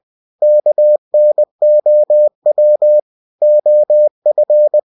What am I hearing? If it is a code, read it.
Morse code, heard as a keyed tone, about 20 wpm, 600 Hz. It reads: KNOW OF